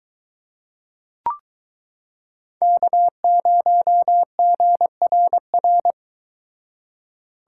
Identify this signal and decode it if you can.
Morse code, heard as a keyed tone, about 23 wpm, 700 Hz. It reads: K0GRR